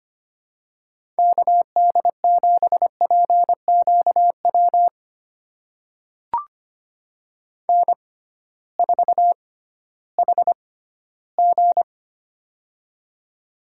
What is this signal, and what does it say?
Morse code, heard as a keyed tone, about 25 wpm, spaced as for 14 wpm, 700 Hz. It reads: KD7PQW N4HG
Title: USA Call Signs - 6x Character Spacing - 25wpm